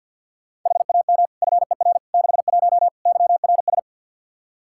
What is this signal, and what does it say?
Morse code, heard as a keyed tone, about 52 wpm, 700 Hz. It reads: HAM FER 61 YRS
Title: Practice QSO - Exchange 3 - More Info - 50wpm